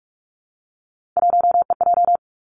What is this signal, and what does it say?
Morse code, heard as a keyed tone, about 46 wpm, 700 Hz. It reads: 1EJ
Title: Three Challenging Letter-Number Combinations 45wpm